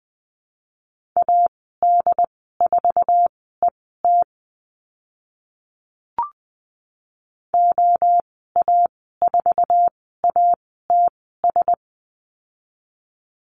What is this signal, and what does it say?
Morse code, heard as a keyed tone, about 20 wpm, spaced as for 14 wpm, 700 Hz. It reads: AD4ET OA4ATS
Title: 069 Lesson 12g - Letter d - Call Signs 20wpm